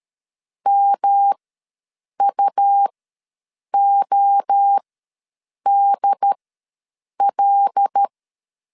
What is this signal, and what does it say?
Morse code, heard as a keyed tone, about 13 wpm, spaced as for 7 wpm, 800 Hz. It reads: MUODL
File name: LEZIONI / 5 LBVC / 5 L ws 12wpm 2958